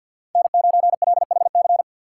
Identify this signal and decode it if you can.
Morse code, heard as a keyed tone, about 51 wpm, 700 Hz. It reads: N9LHC
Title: USA Call Signs - 3x Character Spacing - 50wpm